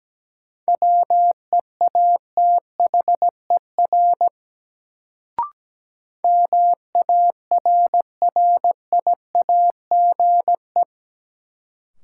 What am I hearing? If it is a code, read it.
Morse code, heard as a keyed tone, about 17 wpm, 700 Hz. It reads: WEATHER MARRIAGE
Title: Sentences from Top 800 Words - New Word Drill - 17wpm